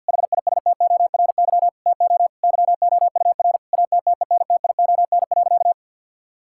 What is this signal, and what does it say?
Morse code, heard as a keyed tone, about 51 wpm, 700 Hz. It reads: HISTORY TO YOUR ATTENTION.
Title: Sentences from Top 1000 Words - Review Part 6 - 50wpm